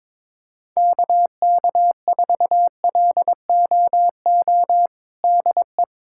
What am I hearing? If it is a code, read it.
Morse code, heard as a keyed tone, about 22 wpm, 700 Hz. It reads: KK4LOO DE
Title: USA Callsign DE Callsign 22wpm